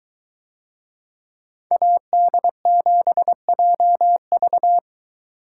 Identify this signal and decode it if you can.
Morse code, heard as a keyed tone, about 23 wpm, 700 Hz. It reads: AD7JV